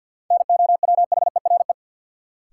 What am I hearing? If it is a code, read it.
Morse code, heard as a keyed tone, about 53 wpm, 700 Hz. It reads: NOWHERE
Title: Sets of 5 Words Encore - New Word Drill - 50wpm